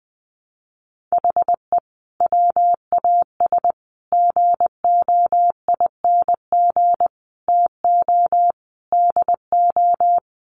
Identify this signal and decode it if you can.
Morse code, heard as a keyed tone, about 20 wpm, 700 Hz. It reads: HE WAS GOING TO DO
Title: Sets of 5 Words Encore2 20wpm